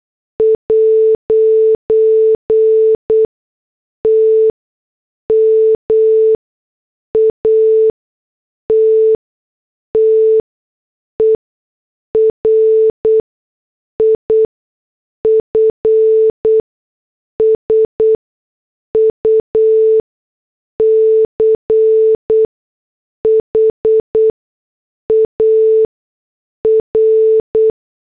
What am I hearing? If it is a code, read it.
Morse code, heard as a keyed tone, about 8 wpm, spaced as for 6 wpm, 440 Hz. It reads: 'T MATTER IF SUCH AR
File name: morse / headlines